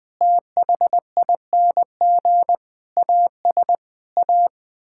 Morse code, heard as a keyed tone, about 20 wpm, 700 Hz. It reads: THING AS A
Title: Sets of 8 Words - 20wpm